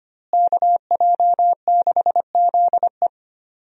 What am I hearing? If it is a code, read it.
Morse code, heard as a keyed tone, about 25 wpm, 700 Hz. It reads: KJ6ZE
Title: USA Call Signs - 5 Characters - Repeated 3x - 25wpm